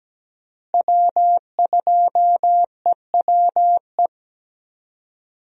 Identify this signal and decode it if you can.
Morse code, heard as a keyed tone, about 17 wpm, 700 Hz. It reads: W2EWE
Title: USA Call Signs - 5 Characters - Repeated 3x - 17wpm